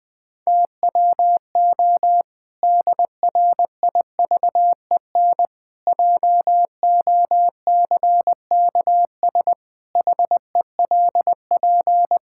Morse code, heard as a keyed tone, about 20 wpm, 700 Hz. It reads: TWO DRIVEN JOCKS HELP
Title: Pangram Challenge v2 - 20wpm